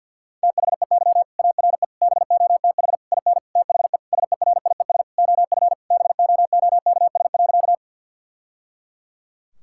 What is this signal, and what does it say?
Morse code, heard as a keyed tone, about 50 wpm, 700 Hz. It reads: THEY ARE BOTH IN THE SERIES OF BOOKS.
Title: Sentences from Top 1000 Words - Review Part 5 - 50wpm